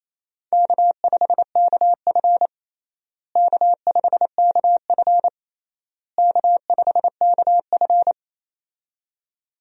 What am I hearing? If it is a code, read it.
Morse code, heard as a keyed tone, about 27 wpm, 700 Hz. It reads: K5KF K5KF K5KF
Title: USA Call Signs - 4 Characters - Repeated 3x - 28wpm